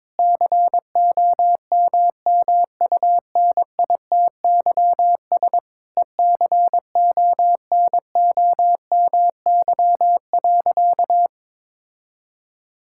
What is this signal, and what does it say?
Morse code, heard as a keyed tone, about 22 wpm, 700 Hz. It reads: COMMUNITYS ECONOMY.
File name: Difficult Sentences with Hints - 22wpm